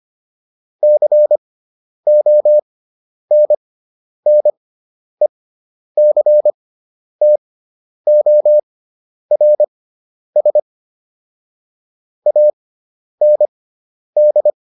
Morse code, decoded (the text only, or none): CONNECTORS AND